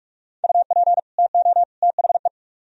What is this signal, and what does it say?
Morse code, heard as a keyed tone, about 41 wpm, 700 Hz. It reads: UP TO THE